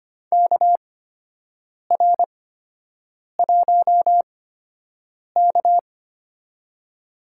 Morse code, heard as a keyed tone, about 25 wpm, spaced as for 7 wpm, 700 Hz. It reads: KR1K
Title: USA Call Signs - 4 Characters - 8x Character Spacing - 25wpm